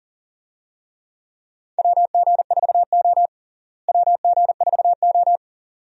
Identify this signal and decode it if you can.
Morse code, heard as a keyed tone, about 40 wpm, 700 Hz. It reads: WG4O WG4O
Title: USA Call Signs - 4 Characters - Repeated 2x - 40wpm